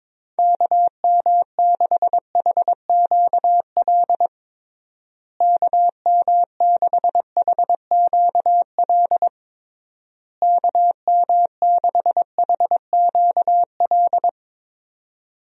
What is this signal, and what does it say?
Morse code, heard as a keyed tone, about 22 wpm, 700 Hz. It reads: KM6HQL KM6HQL KM6HQL
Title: USA Call Signs - 6 Characters - Repeated 3x - 22wpm